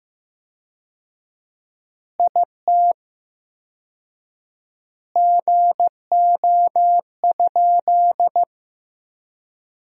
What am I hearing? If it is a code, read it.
Morse code, heard as a keyed tone, about 16 wpm, 700 Hz. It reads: IT GO?
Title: 4x Word-Spacing - Sentences from Top 200 Words - 15wpm